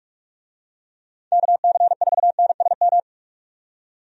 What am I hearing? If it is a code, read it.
Morse code, heard as a keyed tone, about 46 wpm, 700 Hz. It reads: KC4NSM
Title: USA Call Signs - 6 Characters - 45wpm